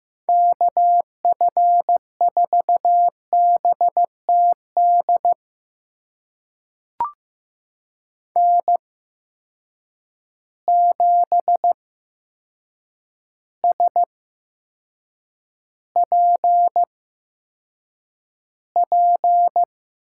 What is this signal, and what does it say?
Morse code, heard as a keyed tone, about 15 wpm, spaced as for 7 wpm, 700 Hz. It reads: KF4BTD N7SPP
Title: USA Call Signs - 8x Character Spacing - 15wpm